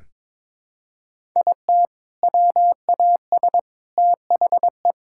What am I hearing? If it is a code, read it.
Morse code, heard as a keyed tone, about 22 wpm, 700 Hz. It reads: IT WAS THE
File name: practice / 22wpm / Build A Phrase - 5-Word Phrases - 22wpm